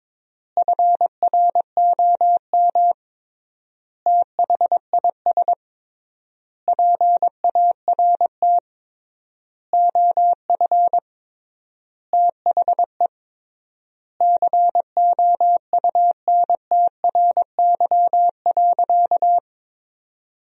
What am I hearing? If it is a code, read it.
Morse code, heard as a keyed tone, about 22 wpm, 700 Hz. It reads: FROM THIS PART OF THE COUNTRY.
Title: 3x Word-Spacing - Sentences from Top 300 Words - 22wpm